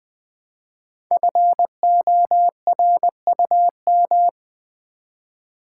Morse code, heard as a keyed tone, about 20 wpm, 700 Hz. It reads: FORUM
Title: Build A Word - 1 Letter At A Time - 5-Letter Words - 20wpm